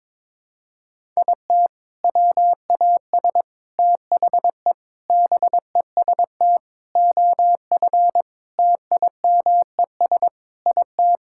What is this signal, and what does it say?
Morse code, heard as a keyed tone, about 23 wpm, 700 Hz. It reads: IT WAS THE BEST OF TIMES IT
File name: practice / 22wpm / Sets of 8 Words - 22wpm